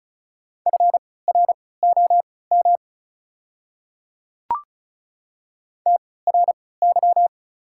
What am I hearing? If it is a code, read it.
Morse code, heard as a keyed tone, about 36 wpm, spaced as for 20 wpm, 700 Hz. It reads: FROM TRY
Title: 3x Character Spacing - Top 200 Words - 35wpm